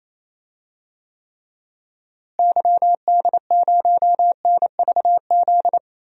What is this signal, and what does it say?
Morse code, heard as a keyed tone, about 27 wpm, 700 Hz. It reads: YD0NVZ
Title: DXCC-100 Call Signs - 6 Characters - Repeated 3x - 28wpm